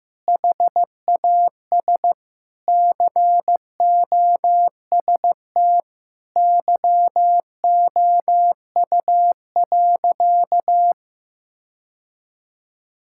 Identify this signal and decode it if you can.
Morse code, heard as a keyed tone, about 15 wpm, 700 Hz. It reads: HAS COST YOU.
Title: Sentences from Top 1000 Words - Review Part 5 - 15wpm